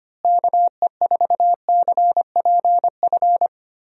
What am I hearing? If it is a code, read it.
Morse code, heard as a keyed tone, about 25 wpm, 700 Hz. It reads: KE4CPF
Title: USA Call Signs - 6 Characters - Repeated 2x - 25wpm